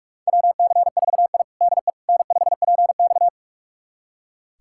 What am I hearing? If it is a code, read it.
Morse code, heard as a keyed tone, about 46 wpm, 700 Hz. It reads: WC4I DE N5PX
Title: USA Callsign DE Callsign - 4 Characters - 45wpm